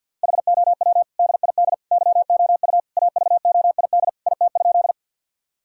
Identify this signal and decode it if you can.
Morse code, heard as a keyed tone, about 50 wpm, 700 Hz. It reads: HOW DID YOU AVOID IT?
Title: Sentences from Top 1000 Words - Review Part 6 - 50wpm